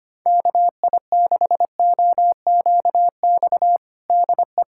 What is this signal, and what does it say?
Morse code, heard as a keyed tone, about 25 wpm, 700 Hz. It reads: KI6OQX DE
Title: USA Callsign DE Callsign 25wpm